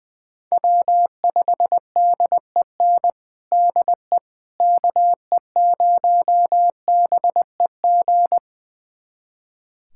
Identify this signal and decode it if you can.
Morse code, heard as a keyed tone, about 21 wpm, 700 Hz. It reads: W5DEN DE KE0BEG
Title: USA Callsign DE Callsign 20wpm